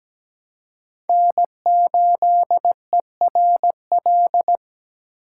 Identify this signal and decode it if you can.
Morse code, heard as a keyed tone, about 17 wpm, 700 Hz. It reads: N8ERL
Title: USA Call Signs - 5 Characters - 2x Character Spacing - 17wpm